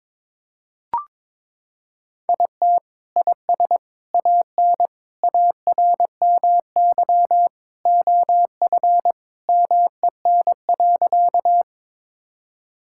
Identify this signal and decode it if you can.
Morse code, heard as a keyed tone, about 21 wpm, 700 Hz. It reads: IT IS AN ARMY OF MEN.
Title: Sentences from Top 1000 Words - Review Part 5 - 22wpm